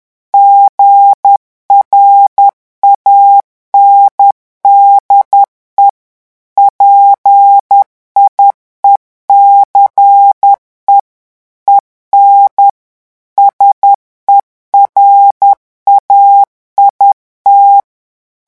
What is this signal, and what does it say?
Morse code, heard as a keyed tone, about 10 wpm, 800 Hz. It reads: GRANDE PIECE EN SERAIT
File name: morse mp3 / revision francais 10 mots minute